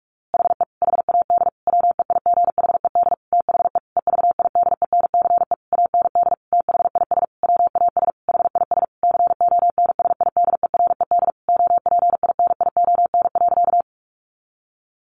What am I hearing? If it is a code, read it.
Morse code, heard as a keyed tone, about 44 wpm, 700 Hz. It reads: HE HAD WEIGHED THE EVIDENCE AND THIS WAS HIS CONSIDERED OPINION.
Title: A Sound Of Thunder 45wpm